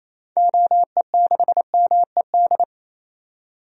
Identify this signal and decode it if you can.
Morse code, heard as a keyed tone, about 28 wpm, 700 Hz. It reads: OE6MED
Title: DXCC-100 Call Signs - 6 Characters - 28wpm